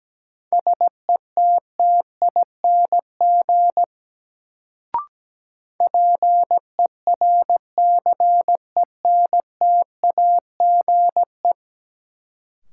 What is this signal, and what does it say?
Morse code, heard as a keyed tone, about 17 wpm, 700 Hz. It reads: SETTING PERCENTAGE
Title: Sentences from Top 1300 Words - New Word Drill - 17wpm